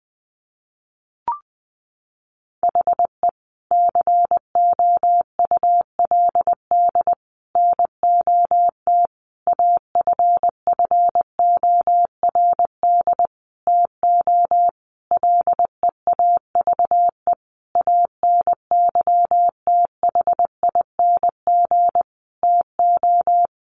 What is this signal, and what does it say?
Morse code, heard as a keyed tone, about 20 wpm, 700 Hz. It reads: HE COULD NOT AFFORD TO LEAVE ANYTHING TO
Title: Sentences from Top 1100 Words 20wpm